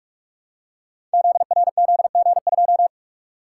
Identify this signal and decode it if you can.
Morse code, heard as a keyed tone, about 44 wpm, 700 Hz. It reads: ZRZG2
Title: Code Groups - 5 Letter:Numbers - 45wpm